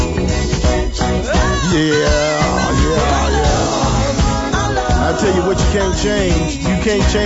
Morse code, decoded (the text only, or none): none